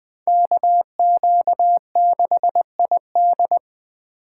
Morse code, heard as a keyed tone, about 21 wpm, 700 Hz. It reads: KQ6ID